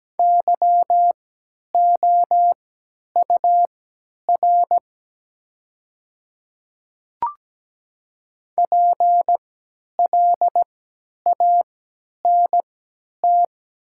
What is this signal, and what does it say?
Morse code, heard as a keyed tone, about 17 wpm, spaced as for 10 wpm, 690 Hz. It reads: YOUR PLANT
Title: 3x Character Spacing - Top 500 Words - 17wpm